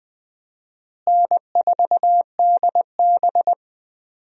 Morse code, heard as a keyed tone, about 20 wpm, 700 Hz. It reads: N4DB
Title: USA Call Signs - 4 Characters - Repeated 2x - 20wpm